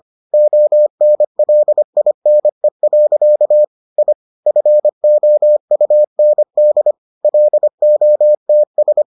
Morse code, heard as a keyed tone, about 25 wpm, 600 Hz. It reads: ONLINE. I FOUND LOTS